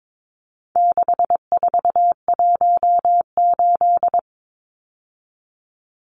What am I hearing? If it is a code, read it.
Morse code, heard as a keyed tone, about 22 wpm, 700 Hz. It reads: 6418
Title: Numbers - 4-Digit 22wpm